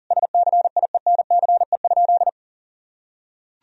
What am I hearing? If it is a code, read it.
Morse code, heard as a keyed tone, about 40 wpm, 700 Hz. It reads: SCIENCE?
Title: Sentences from Top 1000 Words - Review Part 3 - 40wpm